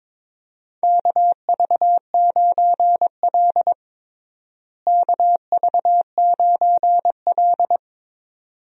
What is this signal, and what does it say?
Morse code, heard as a keyed tone, about 21 wpm, 700 Hz. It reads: KV9L KV9L
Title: USA Call Signs - 4 Characters - Repeated 2x - 22wpm